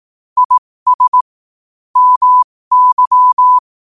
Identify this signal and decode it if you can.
Morse code, heard as a keyed tone, about 18 wpm, spaced as for 15 wpm, 1000 Hz. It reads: IS MY
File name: qotdfiles / 15wpm